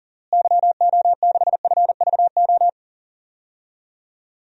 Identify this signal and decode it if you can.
Morse code, heard as a keyed tone, about 40 wpm, 700 Hz. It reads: YO6FVO
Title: DXCC-100 Call Signs - 6 Characters - Repeated 3x - 40wpm